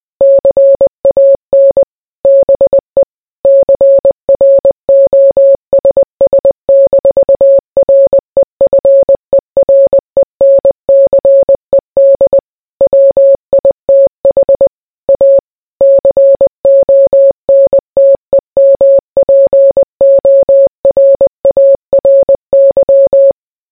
Morse code, heard as a keyed tone, about 20 wpm, 550 Hz. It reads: CAN BE CROSS-REFERENCED WITH A CONTEMPORARY